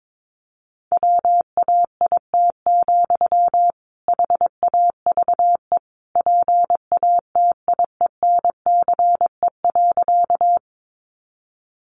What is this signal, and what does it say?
Morse code, heard as a keyed tone, about 22 wpm, 700 Hz. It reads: WAIT, HAVE PATIENCE.
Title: Sentences from Top 1300 Words 22wpm